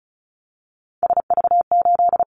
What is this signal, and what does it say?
Morse code, heard as a keyed tone, about 34 wpm, 700 Hz. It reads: SV8
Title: Callsign Prefixes for Top 50 DXCC - 35wpm